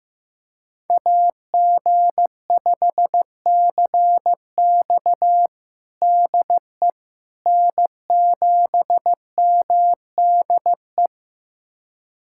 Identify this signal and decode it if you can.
Morse code, heard as a keyed tone, about 15 wpm, 700 Hz. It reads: AG5CX DE N7MDE
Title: USA Callsign DE Callsign - 5 Characters - 15wpm